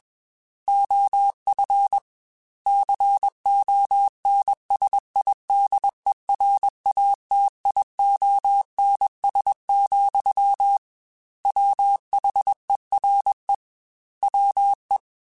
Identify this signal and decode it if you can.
Morse code, heard as a keyed tone, about 21 wpm, 800 Hz. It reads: OF CONSIDERATIONS, WHERE WE
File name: Technical Subjects / Technical Subjects-mp3s / Sw Pwr Sup @20